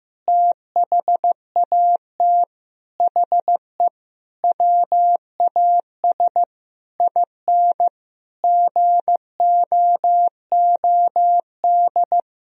Morse code, heard as a keyed tone, about 15 wpm, 700 Hz. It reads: THAT HE WAS IN GOOD
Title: Sentences from Top 1000 Words - Review Part 3 - 15wpm